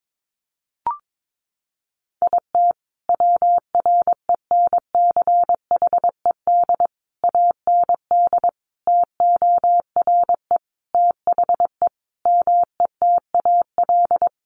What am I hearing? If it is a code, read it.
Morse code, heard as a keyed tone, about 22 wpm, 700 Hz. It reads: IT WRENCHED AND TORE THE METAL
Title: A Sound Of Thunder 22wpm